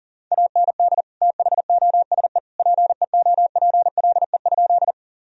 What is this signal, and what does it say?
Morse code, heard as a keyed tone, about 40 wpm, 700 Hz. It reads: AND THOSE PEOPLE?
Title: Sentences from Top 1000 Words - Review Part 1 - 40wpm